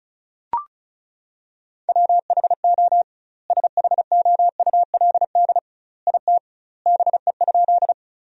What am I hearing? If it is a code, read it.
Morse code, heard as a keyed tone, about 36 wpm, 700 Hz. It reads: WHO SHOULD IT BE?
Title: 2x Word-Spacing - Sentences from Top 300 Words - 35wpm